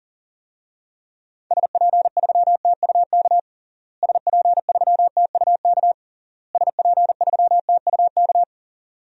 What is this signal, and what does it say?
Morse code, heard as a keyed tone, about 40 wpm, 700 Hz. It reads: SP3TUK SP3TUK SP3TUK
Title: DXCC-100 Call Signs - 6 Characters - Repeated 3x - 40wpm